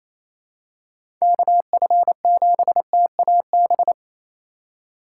Morse code, heard as a keyed tone, about 28 wpm, 700 Hz. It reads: KF7TAB
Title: USA Call Signs - 6 Characters - 4x Character Spacing - 28wpm